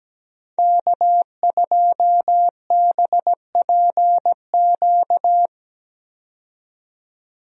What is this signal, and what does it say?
Morse code, heard as a keyed tone, about 17 wpm, 700 Hz. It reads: K2BPQ
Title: USA Call Signs - 5 Characters - 17wpm